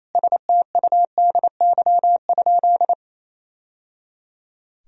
Morse code, heard as a keyed tone, about 28 wpm, 700 Hz. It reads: STUDY?
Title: Sentences from Top 1000 Words - Review Part 2 - 28wpm